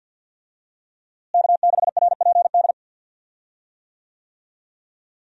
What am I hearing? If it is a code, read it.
Morse code, heard as a keyed tone, about 46 wpm, 700 Hz. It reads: K6RPD
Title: USA Call Signs - 5 Characters - Repeated 2x - 50wpm